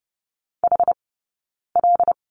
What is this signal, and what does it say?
Morse code, heard as a keyed tone, about 30 wpm, 700 Hz. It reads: H L
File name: Lesson 18d - Letter m - Two Characters 30wpm